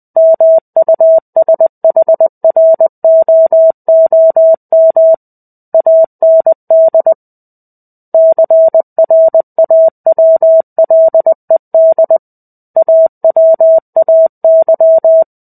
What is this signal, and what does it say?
Morse code, heard as a keyed tone, about 20 wpm, 650 Hz. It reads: MUSHROOM AND CRAWLED AWAY
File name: 05-Chapter-05 20 WPM